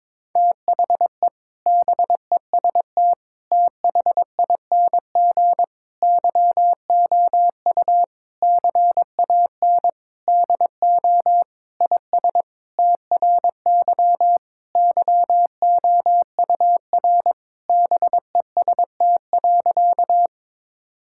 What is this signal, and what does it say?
Morse code, heard as a keyed tone, about 22 wpm, 700 Hz. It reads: THE BEST THING YOU CAN DO IS TRY YOUR BEST.
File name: Sentences with 7-Word Phrases - 22wpm